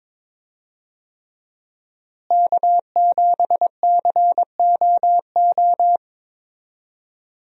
Morse code, heard as a keyed tone, about 21 wpm, 700 Hz. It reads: K7COO
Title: USA Call Signs - 5 Characters - Repeated 3x - 22wpm